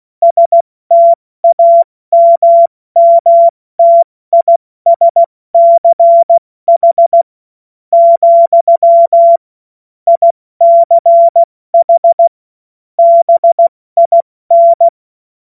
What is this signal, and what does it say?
Morse code, heard as a keyed tone, about 16 wpm, spaced as for 14 wpm, 680 Hz. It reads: STAMMTISCH, ICH BIN
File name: Morsekurs / Witze / Witz 18. Mai 2021, 70 F80